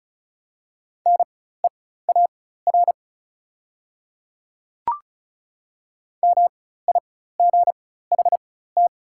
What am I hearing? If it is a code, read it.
Morse code, heard as a keyed tone, about 35 wpm, spaced as for 16 wpm, 690 Hz. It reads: NEAR MIGHT
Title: 4x Character Spacing - Top 500 Words - 35wpm